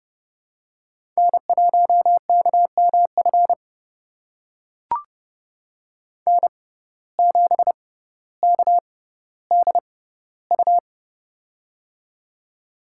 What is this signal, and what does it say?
Morse code, heard as a keyed tone, about 30 wpm, spaced as for 15 wpm, 700 Hz. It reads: N1KMF N7KDU
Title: USA Call Signs - 5 Characters - 6x Character Spacing - 30wpm